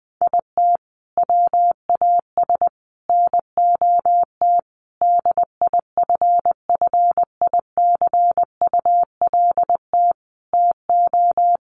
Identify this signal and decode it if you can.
Morse code, heard as a keyed tone, about 19 wpm, 700 Hz. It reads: IT WAS NOT DIFFICULT TO
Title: Sentences from Top 1500 Words 20wpm